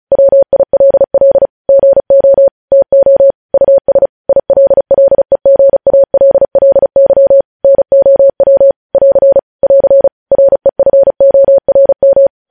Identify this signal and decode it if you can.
Morse code, heard as a keyed tone, about 36 wpm, 550 Hz. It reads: WILL GO TO US ILLEGALLY NOW <AR> <AR> REFORM
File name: podcast / News headlines-1x-35x35